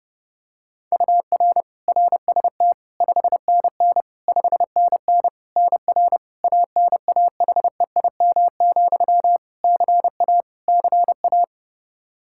Morse code, heard as a keyed tone, about 30 wpm, 700 Hz. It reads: UR RST 5NN 5NN NR ANAHEIM, CA CA
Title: Practice QSO - Exchange 1 - Beginning - 30wpm